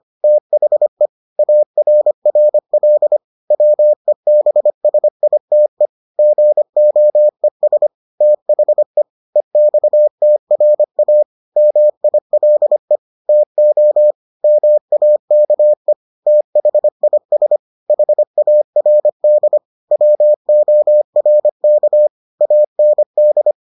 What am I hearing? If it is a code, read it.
Morse code, heard as a keyed tone, about 25 wpm, 600 Hz. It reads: THE ARRL WEBSITE GOES THE EXTRA MILE TO MAKE THIS HARD WORK AND